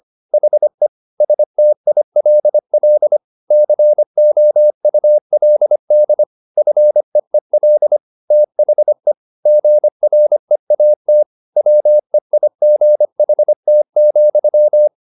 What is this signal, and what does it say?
Morse code, heard as a keyed tone, about 25 wpm, 600 Hz. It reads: HE STILL COULD FEEL THE GREAT WEIGHT,
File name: the old man and the sea part 1 25wpm